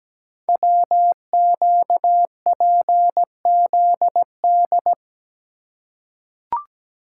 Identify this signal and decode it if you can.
Morse code, heard as a keyed tone, about 17 wpm, 700 Hz. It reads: WQPZD
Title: Code Groups - 5 Letter:Numbers - 17wpm